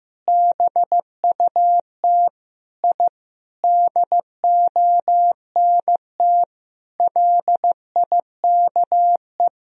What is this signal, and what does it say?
Morse code, heard as a keyed tone, about 15 wpm, 700 Hz. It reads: BUT I DONT LIKE
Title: Sentences from Top 1000 Words - Review Part 3 - 15wpm